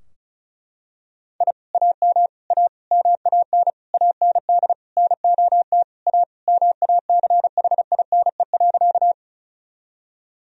Morse code, heard as a keyed tone, about 34 wpm, 700 Hz. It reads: I AM A MAN AND NOT A MACHINE.
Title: Sentences from Top 1000 Words - Review Part 2 - 35wpm